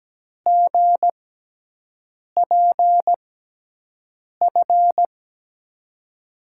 Morse code, heard as a keyed tone, about 17 wpm, spaced as for 6 wpm, 700 Hz. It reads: GPF
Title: USA Call Signs - 5 Characters - 6x Character Spacing - 17wpm